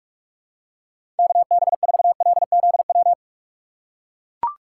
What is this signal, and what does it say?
Morse code, heard as a keyed tone, about 42 wpm, 700 Hz. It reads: KB4LZW
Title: USA Call Signs - 3x Character Spacing - 45wpm